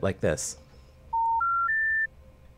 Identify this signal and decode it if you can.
Special information tone, with speech.